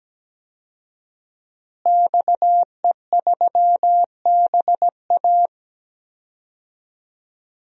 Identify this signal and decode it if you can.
Morse code, heard as a keyed tone, about 17 wpm, 700 Hz. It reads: XE3BA
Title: DXCC-100 Call Signs - 5 Characters - Repeated 3x - 17wpm